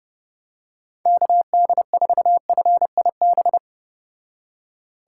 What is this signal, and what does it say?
Morse code, heard as a keyed tone, about 29 wpm, 700 Hz. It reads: KD4FIB